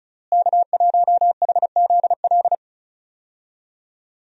Morse code, heard as a keyed tone, about 36 wpm, 700 Hz. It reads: K1HZL
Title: USA Call Signs - 5 Characters - Repeated 3x - 35wpm